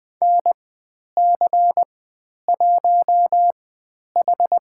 Morse code, heard as a keyed tone, about 20 wpm, spaced as for 10 wpm, 700 Hz. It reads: NC1H
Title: 102 Lesson 17g - Letter u - Call Signs 20wpm